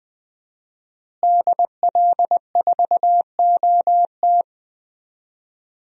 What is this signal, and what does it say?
Morse code, heard as a keyed tone, about 20 wpm, 700 Hz. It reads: DL4OT
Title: USA Call Signs - 5 Characters - Repeated 3x - 20wpm